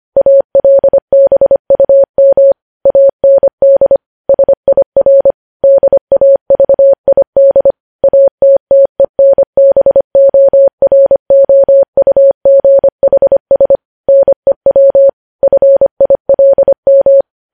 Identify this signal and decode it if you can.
Morse code, heard as a keyed tone, about 25 wpm, 550 Hz. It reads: ALBUM AND SIR DAVID ATTENBOROUGHS NEW FILM
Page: News Headlines in Morse Code at 25 WPM